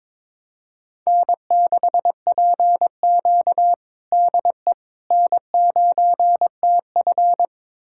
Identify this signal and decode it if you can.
Morse code, heard as a keyed tone, about 22 wpm, 700 Hz. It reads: N6PQ DE N9TF
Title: USA Callsign DE Callsign - 4 Characters - 22wpm